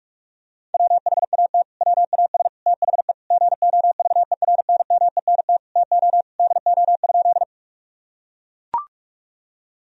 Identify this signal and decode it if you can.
Morse code, heard as a keyed tone, about 41 wpm, 700 Hz. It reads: WHAT WAS THE GOVERNMENT TO DO?
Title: Sentences from Top 1000 Words - Review Part 3 - 45wpm